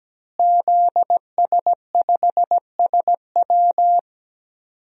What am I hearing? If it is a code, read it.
Morse code, heard as a keyed tone, about 17 wpm, 700 Hz. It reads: ZS5SW
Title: DXCC-100 Call Signs - 5 Characters - Repeated 2x - 17wpm